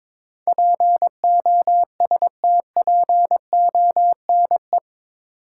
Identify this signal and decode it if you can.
Morse code, heard as a keyed tone, about 23 wpm, 700 Hz. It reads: POSTPONE
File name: practice / 22wpm / Eight Letter Words - 22wpm